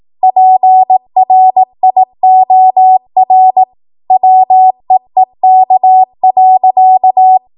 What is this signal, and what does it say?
Morse code, heard as a keyed tone, about 18 wpm, 750 Hz. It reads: PRIOR WEEK.
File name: Archive / 18 WPM / 18WPM